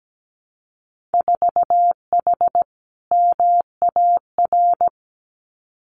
Morse code, heard as a keyed tone, about 17 wpm, 700 Hz. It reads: 4H MAR
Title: Winter Field Day Exchanges Encore 17wpm